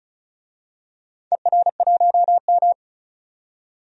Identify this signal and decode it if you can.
Morse code, heard as a keyed tone, about 36 wpm, 700 Hz. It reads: ER1M
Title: DXCC-100 Call Signs - 4 Characters - 35wpm